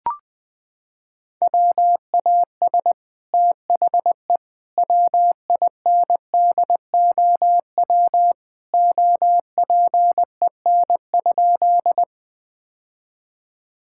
Morse code, heard as a keyed tone, about 21 wpm, 700 Hz. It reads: WAS THE WINDOW OPEN?
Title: Sentences from Top 1600 Words 20wpm